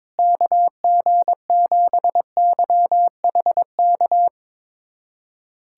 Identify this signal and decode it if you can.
Morse code, heard as a keyed tone, about 22 wpm, 700 Hz. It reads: KG7YHK
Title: USA Callsign DE Callsign 22wpm